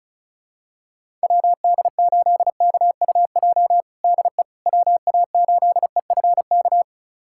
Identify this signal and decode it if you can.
Morse code, heard as a keyed tone, about 36 wpm, 700 Hz. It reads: WD8KUJ DE WA8EFK
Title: USA Callsign DE Callsign 35wpm